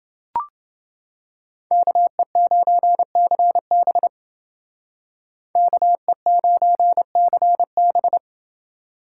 Morse code, heard as a keyed tone, about 28 wpm, 700 Hz. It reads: KE9CB KE9CB